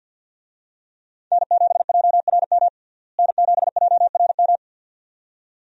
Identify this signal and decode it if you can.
Morse code, heard as a keyed tone, about 52 wpm, 700 Hz. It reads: N7JRM N7JRM
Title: USA Call Signs - 5 Characters - Repeated 2x - 50wpm